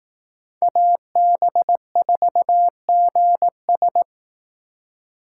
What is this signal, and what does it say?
Morse code, heard as a keyed tone, about 19 wpm, 700 Hz. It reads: AB4GS
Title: Speed Racing - USA 2x2 Call Signs 15wpm